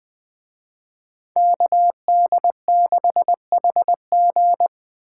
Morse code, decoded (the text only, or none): KD6HG